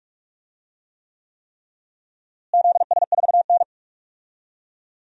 Morse code, heard as a keyed tone, about 46 wpm, 700 Hz. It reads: ZS4N
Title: DXCC-100 Call Signs - 4 Characters - Repeated 3x - 45wpm